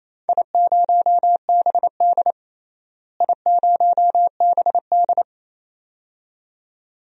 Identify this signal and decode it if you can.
Morse code, heard as a keyed tone, about 28 wpm, 700 Hz. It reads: I0BD I0BD
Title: DXCC-100 Call Signs - 4 Characters - Repeated 2x - 28wpm